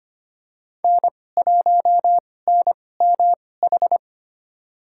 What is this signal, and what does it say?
Morse code, heard as a keyed tone, about 25 wpm, spaced as for 18 wpm, 700 Hz. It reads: N1NMH
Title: USA Call Signs - 5 Characters - 2x Character Spacing - 25wpm